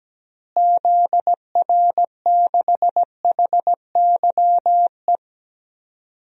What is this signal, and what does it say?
Morse code, heard as a keyed tone, about 17 wpm, 700 Hz. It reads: ZR6HYE
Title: DXCC-100 Call Signs - 6 Characters - Repeated 2x - 17wpm